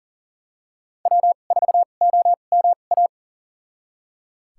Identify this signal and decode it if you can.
Morse code, heard as a keyed tone, about 37 wpm, spaced as for 28 wpm, 700 Hz. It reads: W4OMA
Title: USA Call Signs - 2x Character Spacing - 40wpm